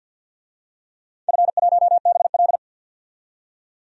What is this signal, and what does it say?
Morse code, heard as a keyed tone, about 46 wpm, 700 Hz. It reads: F1BL
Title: DXCC-100 Call Signs - 4 Characters - 50wpm